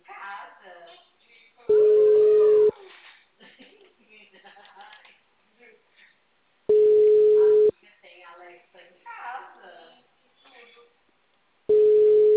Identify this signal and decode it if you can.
Ringback tone (Europe, 425 Hz), with speech.
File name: out- -2005- - - .52425